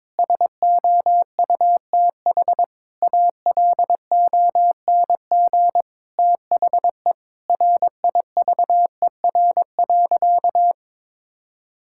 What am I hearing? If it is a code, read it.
Morse code, heard as a keyed tone, about 22 wpm, 700 Hz. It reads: SOUTH ALONG THE RIVER.
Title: Sentences from Top 1000 Words - Review Part 6 - 22wpm